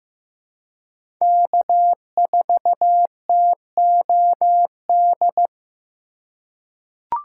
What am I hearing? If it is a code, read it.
Morse code, heard as a keyed tone, about 15 wpm, 700 Hz. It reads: K4TOD
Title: USA Call Signs - 5 Characters - 3x Character Spacing - 15wpm